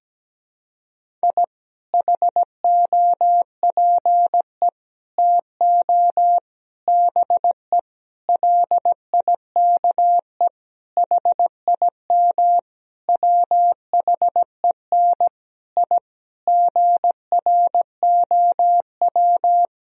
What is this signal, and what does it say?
Morse code, heard as a keyed tone, about 17 wpm, 700 Hz. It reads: I HOPE TO BE LIKE HIM WHEN I GROW
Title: Sentences from Top 1000 Words - Review Part 1 - 17wpm